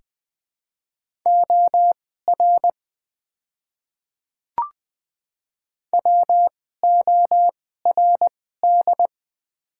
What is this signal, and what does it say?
Morse code, heard as a keyed tone, about 20 wpm, spaced as for 14 wpm, 700 Hz. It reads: OR WORD
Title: 2x Character Spacing - Top 200 Words - 20wpm